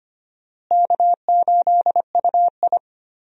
Morse code, heard as a keyed tone, about 25 wpm, 700 Hz. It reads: K8UI
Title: USA Call Signs - 4 Characters - Repeated 3x - 25wpm